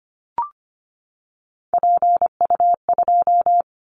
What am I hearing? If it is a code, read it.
Morse code, heard as a keyed tone, about 25 wpm, 700 Hz. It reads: PU2
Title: Callsign Prefixes for Top 100 DXCC - 25wpm